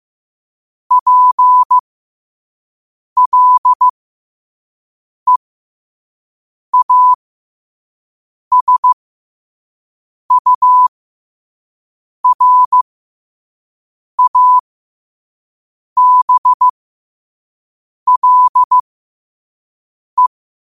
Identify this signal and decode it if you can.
Morse code, heard as a keyed tone, about 15 wpm, spaced as for 5 wpm, 1000 Hz. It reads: PLEASURABLE
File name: qotdfiles / 5wpm